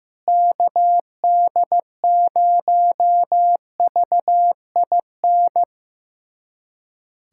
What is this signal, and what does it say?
Morse code, heard as a keyed tone, about 15 wpm, 700 Hz. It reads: KD0VIN
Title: USA Callsign DE Callsign 15wpm